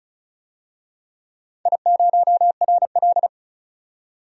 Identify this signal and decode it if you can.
Morse code, heard as a keyed tone, about 36 wpm, 700 Hz. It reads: I0RL
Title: DXCC-100 Call Signs - 4 Characters - 35wpm